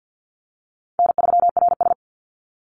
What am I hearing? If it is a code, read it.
Morse code, heard as a keyed tone, about 48 wpm, 700 Hz. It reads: N3RS
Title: Easy Call Signs - 50wpm